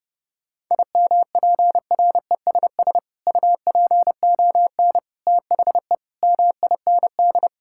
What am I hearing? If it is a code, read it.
Morse code, heard as a keyed tone, about 30 wpm, 700 Hz. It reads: IMPRESS UPON THE MIND
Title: Sentences from Top 1600 Words 30wpm